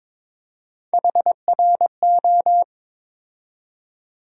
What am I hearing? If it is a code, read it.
Morse code, heard as a keyed tone, about 22 wpm, 700 Hz. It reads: HRO
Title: USA Call Sign Postfixes - 3 Letters 22wpm